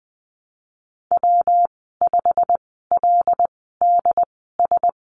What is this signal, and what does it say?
Morse code, heard as a keyed tone, about 20 wpm, spaced as for 15 wpm, 700 Hz. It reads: W5LDS
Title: 128 Lesson 21g - Number 6 - Call Signs 20wpm